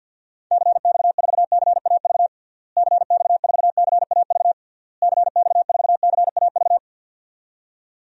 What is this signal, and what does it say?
Morse code, heard as a keyed tone, about 52 wpm, 700 Hz. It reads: CX4CAV CX4CAV CX4CAV
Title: DXCC-100 Call Signs - 6 Characters - Repeated 3x - 50wpm